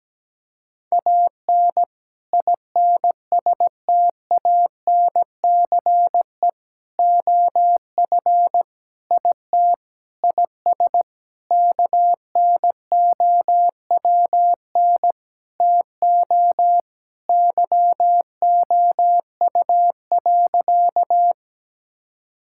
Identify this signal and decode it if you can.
Morse code, heard as a keyed tone, about 17 wpm, 700 Hz. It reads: AN INSTANCE OF IT IS KNOWN TO YOU.
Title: Sentences from Top 1000 Words - Review Part 5 - 17wpm